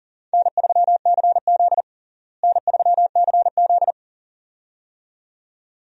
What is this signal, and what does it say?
Morse code, heard as a keyed tone, about 40 wpm, 700 Hz. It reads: N3CZ N3CZ
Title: USA Call Signs - 4 Characters - Repeated 2x - 40wpm